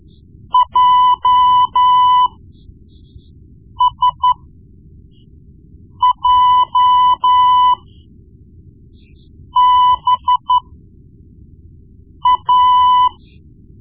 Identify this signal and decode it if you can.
Morse code, heard as a keyed tone, about 13 wpm, spaced as for 4 wpm, 1000 Hz. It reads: JSJBA
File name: Morse Lessons / Morse Lessons / Tape-1B